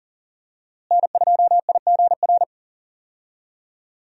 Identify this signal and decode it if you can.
Morse code, heard as a keyed tone, about 40 wpm, 700 Hz. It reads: N2IGR
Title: USA Call Signs - 5 Characters - Repeated 2x - 40wpm